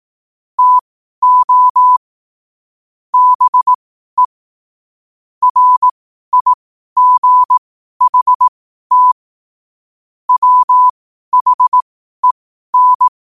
Morse code, heard as a keyed tone, about 18 wpm, spaced as for 12 wpm, 1000 Hz. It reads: TO BE RIGHT WHEN
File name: qotdfiles / 12wpm